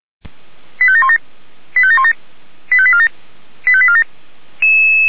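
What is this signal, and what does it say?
Five-tone selective call (ZVEI), heard as quick runs of five tones, tones of about 80 ms. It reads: 85717 85717 85847 85847
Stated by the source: ZVEI five-tone selective call received via SDR.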